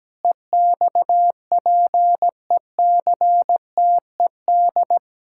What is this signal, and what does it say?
Morse code, heard as a keyed tone, about 17 wpm, 700 Hz. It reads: EXPECTED
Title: Pangram Challenge v2 - 17wpm